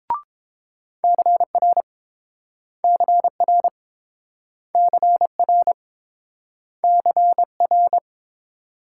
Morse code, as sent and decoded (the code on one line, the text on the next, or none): -.-. .-. / -.-. .-. / -.-. .-. / -.-. .-.
CR CR CR CR